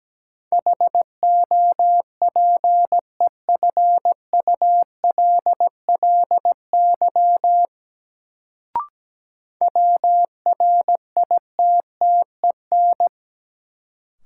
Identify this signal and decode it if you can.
Morse code, heard as a keyed tone, about 17 wpm, 700 Hz. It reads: HOPEFULLY WRITTEN
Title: Sentences from Top 1500 Words - New Word Drill - 17wpm